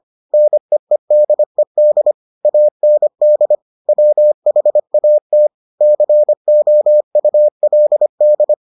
Morse code, heard as a keyed tone, about 25 wpm, 600 Hz. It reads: NEEDED AND WHAT COULD